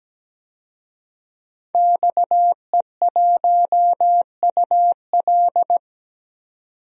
Morse code, heard as a keyed tone, about 17 wpm, 700 Hz. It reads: XE1UL
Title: DXCC-100 Call Signs - 5 Characters - 17wpm